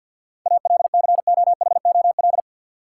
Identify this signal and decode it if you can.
Morse code, heard as a keyed tone, about 53 wpm, 700 Hz. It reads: ALCOHOL
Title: Sentences from Top 1400 Words - New Word Drill - 50wpm